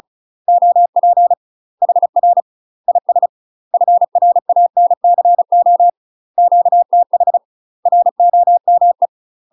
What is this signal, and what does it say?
Morse code, heard as a keyed tone, about 34 wpm, 700 Hz. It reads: OP HR IS FRANCO QTH ROME
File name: UK MASTER QSOs TRAINING / 35WPM / UK MASTER QSOs 0001